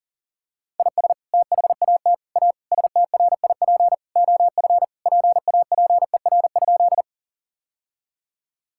Morse code, heard as a keyed tone, about 37 wpm, 700 Hz. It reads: IS THAT A STRIP OF PAPER?
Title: Sentences from Top 1600 Words 40wpm